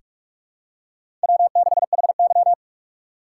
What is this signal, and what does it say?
Morse code, heard as a keyed tone, about 46 wpm, 700 Hz. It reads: W6HY
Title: USA Call Signs - 4 Characters - 6x Character Spacing - 45wpm